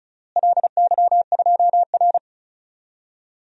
Morse code, heard as a keyed tone, about 34 wpm, 700 Hz. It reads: LY2R
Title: DXCC-100 Call Signs - 4 Characters - Repeated 2x - 35wpm